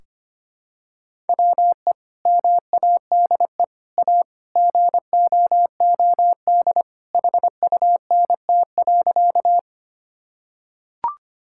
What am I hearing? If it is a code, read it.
Morse code, heard as a keyed tone, about 25 wpm, 700 Hz. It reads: WE MADE A GOOD HUNT.
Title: Sentences from Top 1600 Words 25wpm